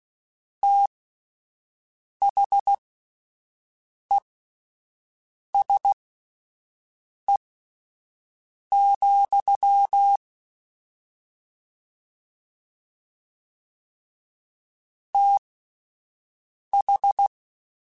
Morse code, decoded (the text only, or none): THESE, TH